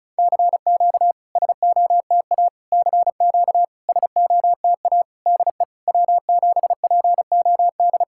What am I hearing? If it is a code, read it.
Morse code, heard as a keyed tone, about 36 wpm, 690 Hz. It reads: CQ SOTA CQ SOTA DE W7POD
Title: SOTA QSO Practice 35wpm